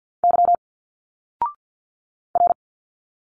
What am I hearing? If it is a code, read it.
Morse code, heard as a keyed tone, about 51 wpm, 700 Hz. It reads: Y R
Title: Single Letters 50wpm